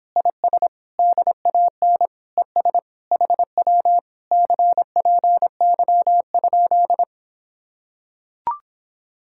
Morse code, heard as a keyed tone, about 26 wpm, 700 Hz. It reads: IS DAN ES HW CPY?